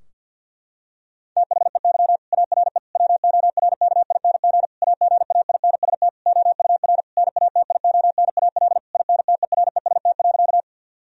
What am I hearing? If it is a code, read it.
Morse code, heard as a keyed tone, about 50 wpm, 700 Hz. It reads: THEY ARE WORKING AGAINST OUR NATIONAL INTEREST.
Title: Sentences from Top 1000 Words - Review Part 4 - 50wpm